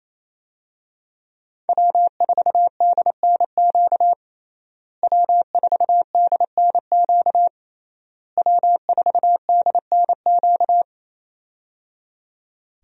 Morse code, heard as a keyed tone, about 28 wpm, 700 Hz. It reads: W4DNQ W4DNQ W4DNQ
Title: USA Call Signs - 5 Characters - Repeated 3x - 28wpm